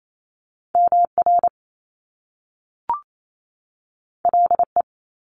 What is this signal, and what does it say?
Morse code, heard as a keyed tone, about 28 wpm, 700 Hz. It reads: MR LE
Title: Top Two Letter Combinations in Words 28wpm